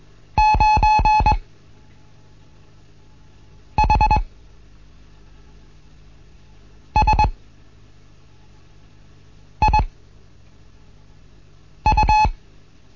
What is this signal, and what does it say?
Morse code, heard as a keyed tone, about 21 wpm, spaced as for 4 wpm, 860 Hz. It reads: 9HSIU
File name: Morsen / Kurs A 15